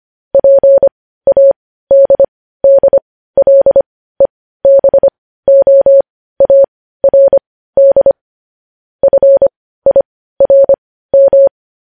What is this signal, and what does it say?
Morse code, heard as a keyed tone, about 25 wpm, spaced as for 15 wpm, 550 Hz. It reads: PADDLEBOARD FIRM
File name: podcast / News headlines-1x-25x15